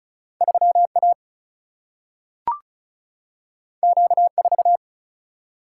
Morse code, heard as a keyed tone, about 36 wpm, 700 Hz. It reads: 3A Q4
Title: Single-Letter Single-Number 35wpm